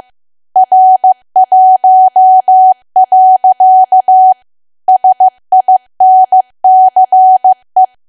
Morse code, heard as a keyed tone, about 15 wpm, 750 Hz. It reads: R1. SINCE
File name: cwa / W1AW-Level2 / 15WPM